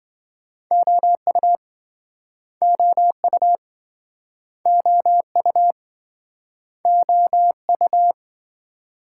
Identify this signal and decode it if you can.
Morse code, heard as a keyed tone, about 26 wpm, 700 Hz. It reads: OU OU OU OU